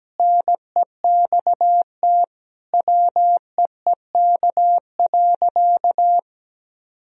Morse code, heard as a keyed tone, about 17 wpm, 700 Hz. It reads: NEXT WEEK.
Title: Sentences from Top 1000 Words - Review Part 4 - 17wpm